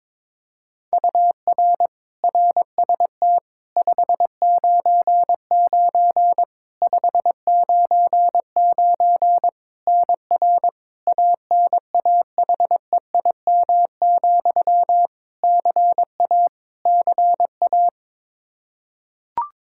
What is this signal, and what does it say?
Morse code, heard as a keyed tone, about 22 wpm, 700 Hz. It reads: UR RST 599 599 NR ANAHEIM, CA CA
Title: Practice QSO - Exchange 1 - Beginning - 22wpm